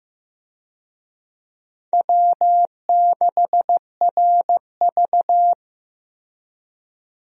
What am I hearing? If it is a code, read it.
Morse code, heard as a keyed tone, about 15 wpm, 700 Hz. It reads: W6RV
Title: USA Call Signs - 4 Characters - 15wpm